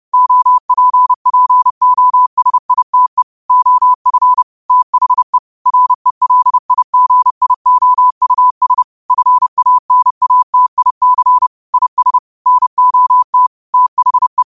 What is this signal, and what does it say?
Morse code, heard as a keyed tone, about 30 wpm, 1000 Hz. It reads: OPPOSITE OF THE RELIGIOUS FANATIC IS NOT THE
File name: qotdfiles / 30wpm